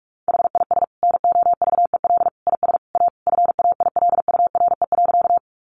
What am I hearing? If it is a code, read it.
Morse code, heard as a keyed tone, about 46 wpm, 700 Hz. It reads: HIS NOVEL IS A FAILURE.
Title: Sentences from Top 1300 Words 45wpm